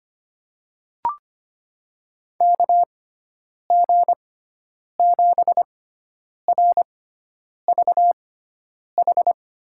Morse code, heard as a keyed tone, about 25 wpm, spaced as for 9 wpm, 700 Hz. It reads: KG7RVH
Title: USA Call Signs - 6 Characters - 6x Character Spacing - 25wpm